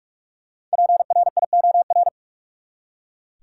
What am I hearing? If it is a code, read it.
Morse code, heard as a keyed tone, about 45 wpm, 700 Hz. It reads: PRIOR
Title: Five Letter Words - 45wpm